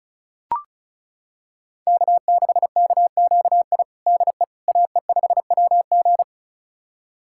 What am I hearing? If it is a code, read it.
Morse code, heard as a keyed tone, about 36 wpm, 690 Hz. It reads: K6KQI DE AE5WG
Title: USA Callsign DE Callsign - 5 Characters - 35wpm